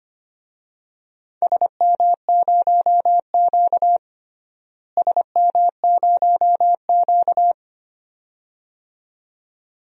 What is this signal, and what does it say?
Morse code, heard as a keyed tone, about 25 wpm, 700 Hz. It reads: SM0Q SM0Q
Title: DXCC-100 Call Signs - 4 Characters - Repeated 2x - 25wpm